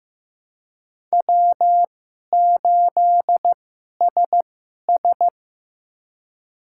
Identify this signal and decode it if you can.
Morse code, heard as a keyed tone, about 15 wpm, spaced as for 11 wpm, 700 Hz. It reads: W8SS
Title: USA Call Signs - 4 Characters - 2x Character Spacing - 15wpm